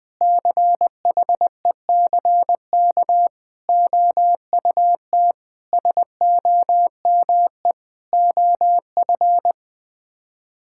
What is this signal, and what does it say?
Morse code, heard as a keyed tone, about 21 wpm, 700 Hz. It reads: CHECK OUT SOME OF
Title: Sets of 9 Words - 20wpm